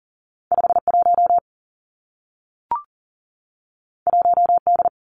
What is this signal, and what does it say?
Morse code, heard as a keyed tone, about 41 wpm, 700 Hz. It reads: 5 1 1 D
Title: Two Challenging Letter-Number Combinations 40wpm